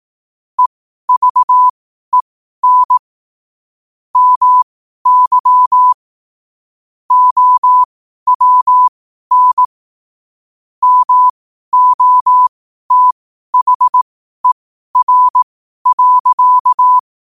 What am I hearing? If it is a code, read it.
Morse code, heard as a keyed tone, about 18 wpm, spaced as for 12 wpm, 1000 Hz. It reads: EVEN MY OWN MOTHER.
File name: qotdfiles / 12wpm